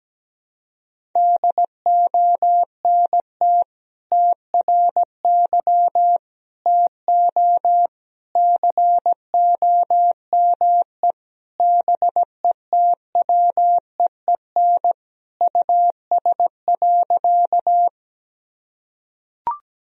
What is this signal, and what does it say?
Morse code, heard as a keyed tone, about 17 wpm, 700 Hz. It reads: DONT TRY TO COME BETWEEN US.
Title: Sentences from Top 1000 Words - Review Part 1 - 17wpm